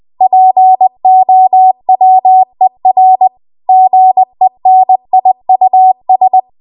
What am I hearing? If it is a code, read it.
Morse code, heard as a keyed tone, about 20 wpm, 750 Hz. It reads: POWER GENIUS
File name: Archive / 20 WPM / 20WPM